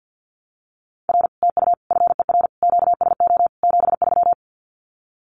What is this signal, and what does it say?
Morse code, heard as a keyed tone, about 50 wpm, 700 Hz. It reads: R TU FER QSO 73
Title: Common QSO Phrases Encore 50wpm